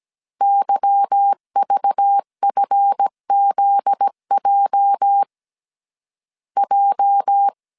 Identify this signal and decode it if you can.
Morse code, heard as a keyed tone, about 17 wpm, 800 Hz. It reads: YVFZJ J
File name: CW / DET / 1 9 1500C S C 16wpm 2909